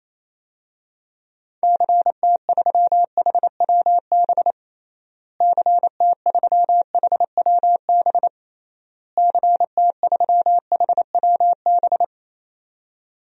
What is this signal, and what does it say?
Morse code, heard as a keyed tone, about 28 wpm, 700 Hz. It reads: CT3HWB CT3HWB CT3HWB
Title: DXCC-100 Call Signs - 6 Characters - Repeated 3x - 28wpm